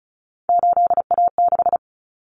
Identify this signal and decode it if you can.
Morse code, heard as a keyed tone, about 34 wpm, 700 Hz. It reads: 8A6
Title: Common Cut Numbers - 3-Digit 35wpm